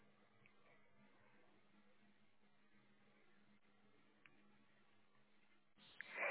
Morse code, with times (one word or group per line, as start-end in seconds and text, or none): none